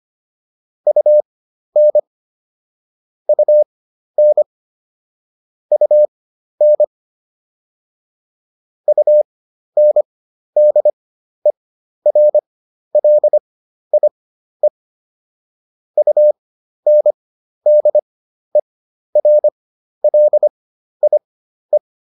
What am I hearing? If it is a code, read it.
Morse code, heard as a keyed tone, about 25 wpm, spaced as for 12 wpm, 600 Hz. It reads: UN UN UN UNDERLIE UNDERLIE